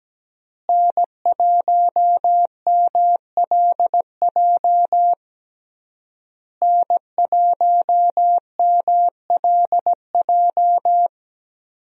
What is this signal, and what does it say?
Morse code, heard as a keyed tone, about 17 wpm, 700 Hz. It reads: N1MLJ N1MLJ
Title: USA Call Signs - 5 Characters - Repeated 3x - 17wpm